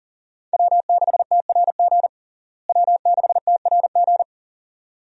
Morse code, heard as a keyed tone, about 40 wpm, 700 Hz. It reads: W6TRG W6TRG
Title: USA Call Signs - 5 Characters - Repeated 2x - 40wpm